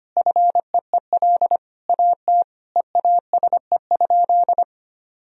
Morse code, heard as a keyed tone, about 25 wpm, 700 Hz. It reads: FEEL AT EASE?
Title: Sentences from Top 1000 Words - Review Part 1 - 25wpm